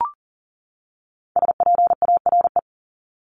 Morse code, heard as a keyed tone, about 40 wpm, 700 Hz. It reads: SPARE
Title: Top 1751-2000 Words 40wpm